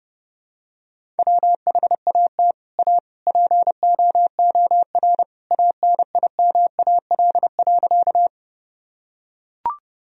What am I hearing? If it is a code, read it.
Morse code, heard as a keyed tone, about 30 wpm, 700 Hz. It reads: WHAT A POOR ANIMAL.
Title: Sentences from Top 1000 Words - Review Part 1 - 30wpm